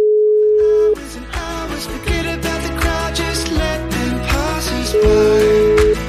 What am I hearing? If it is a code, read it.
Ringback tone (Europe, 425 Hz), with noise.